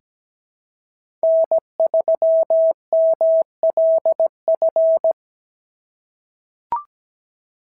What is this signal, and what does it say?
Morse code, heard as a keyed tone, about 17 wpm, 690 Hz. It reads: N3MLF
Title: SOTA QSO Practice 17wpm